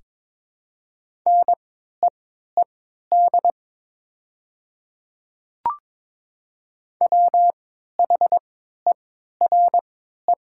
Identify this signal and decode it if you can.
Morse code, heard as a keyed tone, about 22 wpm, spaced as for 13 wpm, 700 Hz. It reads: NEED WHERE
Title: 3x Character Spacing - Top 200 Words - 22wpm